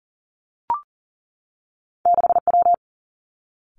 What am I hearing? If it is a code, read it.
Morse code, heard as a keyed tone, about 39 wpm, 700 Hz. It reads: 6W
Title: Two Challenging Letter-Number Combinations 40wpm